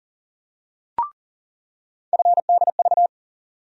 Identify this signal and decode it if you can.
Morse code, heard as a keyed tone, about 40 wpm, 700 Hz. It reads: FDV
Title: USA Call Sign Postfixes - 3 Letters 40wpm